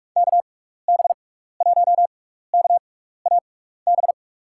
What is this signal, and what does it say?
Morse code, heard as a keyed tone, about 41 wpm, spaced as for 15 wpm, 700 Hz. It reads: KB1KAB
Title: USA Call Signs - 6 Characters - 6x Character Spacing - 45wpm